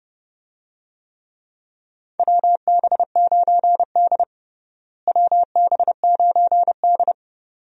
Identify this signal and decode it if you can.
Morse code, heard as a keyed tone, about 30 wpm, 700 Hz. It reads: WB9D WB9D
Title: USA Call Signs - 4 Characters - Repeated 2x - 30wpm